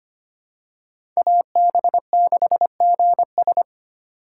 Morse code, heard as a keyed tone, about 25 wpm, 700 Hz. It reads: AB6GS
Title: USA Call Signs - 5 Characters - 8x Character Spacing - 25wpm